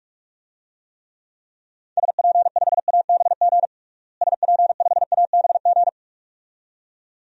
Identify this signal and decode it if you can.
Morse code, heard as a keyed tone, about 42 wpm, 700 Hz. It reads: SP5ABG SP5ABG
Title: DXCC-100 Call Signs - 6 Characters - Repeated 2x - 45wpm